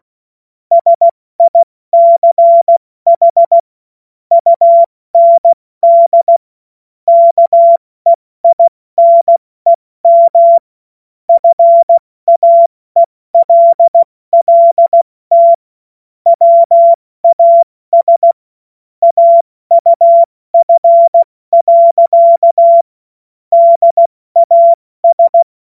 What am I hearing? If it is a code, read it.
Morse code, heard as a keyed tone, about 16 wpm, 680 Hz. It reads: SICH UND KEINEM FAELLT WAS AUF. DAS
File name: Morsekurs / Witze / Witz 08. Februar 2022 70 F80